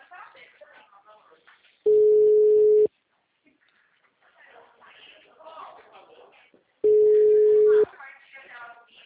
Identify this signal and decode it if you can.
Ringback tone (Europe, 425 Hz), with speech.
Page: Leme 2